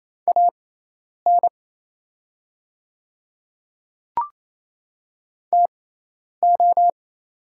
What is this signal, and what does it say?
Morse code, heard as a keyed tone, about 28 wpm, spaced as for 10 wpm, 700 Hz. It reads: AN TO
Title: 6x Character Spacing - Top 100 Words - 28wpm